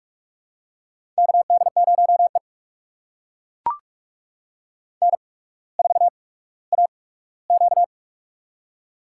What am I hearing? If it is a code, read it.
Morse code, heard as a keyed tone, about 42 wpm, spaced as for 19 wpm, 690 Hz. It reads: KD0E N4AQ
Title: USA Call Signs - 4 Characters - 8x Character Spacing - 45wpm